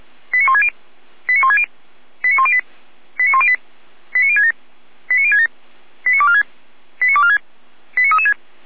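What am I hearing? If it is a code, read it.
Five-tone selective call (ZVEI), heard as quick runs of five tones, tones of about 70 ms. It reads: 89160 89160 89108 89108 89067 89067 89246 89246 89336
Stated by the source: ZVEI five-tone selective call 89099 received with an SDR, from the Feuerwehr Orlamünde audio server.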